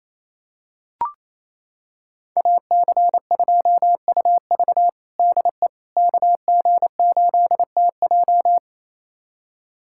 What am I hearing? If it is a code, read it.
Morse code, heard as a keyed tone, about 28 wpm, 700 Hz. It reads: AC2UV DE KG8TJ
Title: USA Callsign DE Callsign - 5 Characters - 28wpm